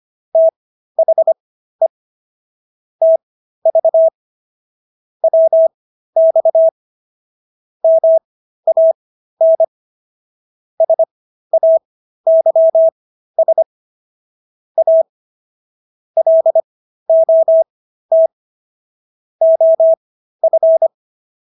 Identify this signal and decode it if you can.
Morse code, heard as a keyed tone, about 25 wpm, spaced as for 13 wpm, 650 Hz. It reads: THE TV WX MAN SAYS A LOT OF